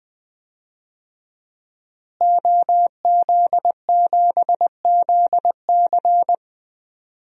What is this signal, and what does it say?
Morse code, heard as a keyed tone, about 20 wpm, 700 Hz. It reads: OZ7ZC
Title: DXCC-100 Call Signs - 5 Characters - Repeated 2x - 20wpm